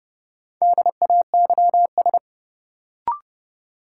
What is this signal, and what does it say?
Morse code, heard as a keyed tone, about 30 wpm, 700 Hz. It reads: DAYS
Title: Sets of 5 Words Encore - New Word Drill - 30wpm